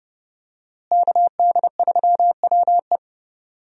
Morse code, heard as a keyed tone, about 28 wpm, 700 Hz. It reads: KD3WE
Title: USA Call Signs - 5 Characters - Repeated 3x - 30wpm